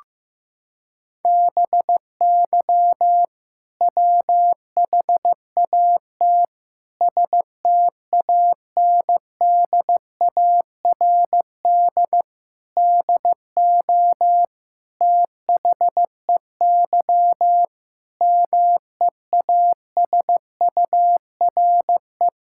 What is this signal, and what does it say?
Morse code, heard as a keyed tone, about 15 wpm, 700 Hz. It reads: BY WHAT STANDARD DO THEY MEASURE
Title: Sentences from Top 1000 Words - Review Part 6 - 15wpm